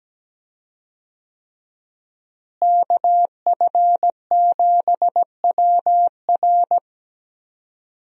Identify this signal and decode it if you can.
Morse code, heard as a keyed tone, about 17 wpm, 700 Hz. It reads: KF7WR
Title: USA Call Signs - 5 Characters - Repeated 3x - 17wpm